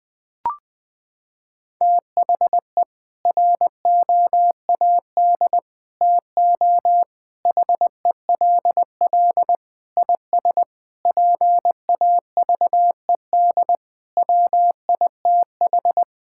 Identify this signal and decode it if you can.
Morse code, heard as a keyed tone, about 20 wpm, 700 Hz. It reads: THE ROAD TO HELL IS PAVED WITH
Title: Sets of 8 Words - 20wpm